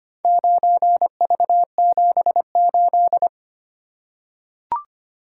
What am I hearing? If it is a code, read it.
Morse code, heard as a keyed tone, about 25 wpm, 700 Hz. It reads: 9V78
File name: practice / 25wpm / Four Letter-Number Combinations - 25wpm